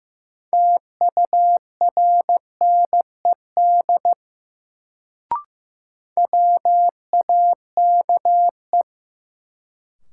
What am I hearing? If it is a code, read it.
Morse code, heard as a keyed tone, about 15 wpm, 700 Hz. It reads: TURNED WAKE
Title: Sets of 4 Words - New Word Drill - 15wpm